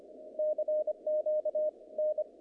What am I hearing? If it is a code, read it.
Morse code, heard as a keyed tone, about 24 wpm, 610 Hz, with radio noise. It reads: CQ N
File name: contests / naqp-cw-k5tr-op-k5pi- -2200